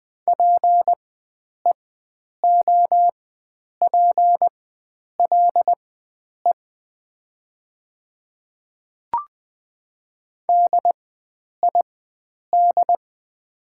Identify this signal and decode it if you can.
Morse code, heard as a keyed tone, about 20 wpm, spaced as for 9 wpm, 700 Hz. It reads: PEOPLE DID
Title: 4x Character Spacing - Top 100 Words - 20wpm